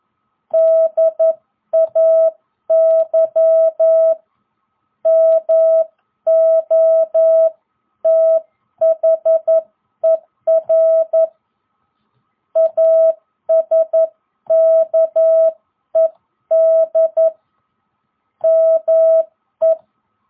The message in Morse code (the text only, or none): DAY MOTHER ASKED ME